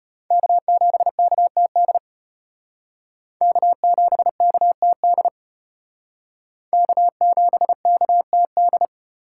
Morse code, heard as a keyed tone, about 32 wpm, 700 Hz. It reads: K7KTD K7KTD K7KTD